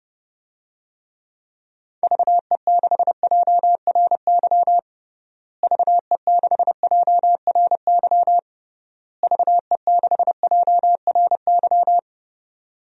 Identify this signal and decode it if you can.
Morse code, heard as a keyed tone, about 32 wpm, 700 Hz. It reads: VE6JRY VE6JRY VE6JRY
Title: DXCC-100 Call Signs - 6 Characters - Repeated 3x - 30wpm